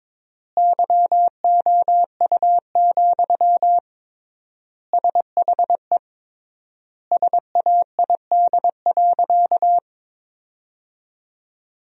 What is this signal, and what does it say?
Morse code, heard as a keyed tone, about 22 wpm, 700 Hz. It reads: YOU, SHE SAID.
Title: 3x Word-Spacing - Sentences from Top 400 Words - 22wpm